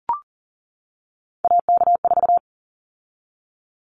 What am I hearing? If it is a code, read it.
Morse code, heard as a keyed tone, about 40 wpm, 700 Hz. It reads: AK4
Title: Callsign Prefixes for Top 25 DXCC - 40wpm